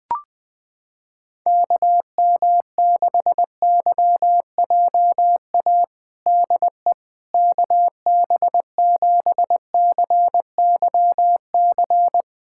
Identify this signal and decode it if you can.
Morse code, heard as a keyed tone, about 20 wpm, 700 Hz. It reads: KM6YJA DE KB7CYC
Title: USA Callsign DE Callsign - 6 Characters - 20wpm